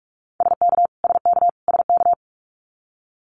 Morse code, heard as a keyed tone, about 44 wpm, 700 Hz. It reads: SK SK SK
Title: Canadian Provinces and Territories - Repeated 3x - 45wpm